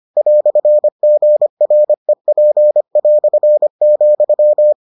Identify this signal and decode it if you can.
Morse code, heard as a keyed tone, about 25 wpm, 600 Hz. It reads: "GREP",